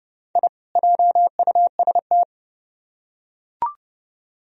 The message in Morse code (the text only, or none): I JUST